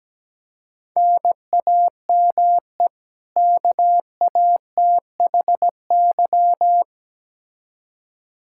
Morse code, heard as a keyed tone, about 17 wpm, 700 Hz. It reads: NAME KATHY
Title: Practice QSO - Exchange 1 - Beginning - 17wpm